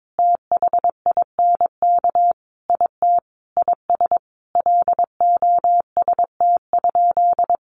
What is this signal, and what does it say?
Morse code, heard as a keyed tone, about 22 wpm, 700 Hz. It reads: THINK IT IS LOST?
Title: Sentences from Top 1400 Words 22wpm